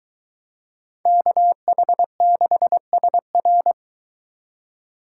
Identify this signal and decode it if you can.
Morse code, heard as a keyed tone, about 24 wpm, 700 Hz. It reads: KH6SR